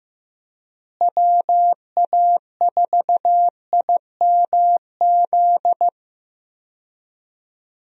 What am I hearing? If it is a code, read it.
Morse code, heard as a keyed tone, about 15 wpm, 700 Hz. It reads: WA4IMZ
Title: USA Call Signs - 6 Characters - Repeated 3x - 15wpm